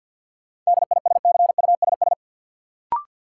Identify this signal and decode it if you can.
Morse code, heard as a keyed tone, about 53 wpm, 700 Hz. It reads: DISCUSS
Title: Sentences from Top 1000 Words - New Word Drill - 50wpm